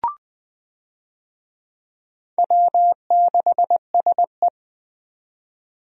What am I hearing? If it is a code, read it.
Morse code, heard as a keyed tone, about 20 wpm, 700 Hz. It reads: W6SE
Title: USA Call Signs - 4 Characters - 20wpm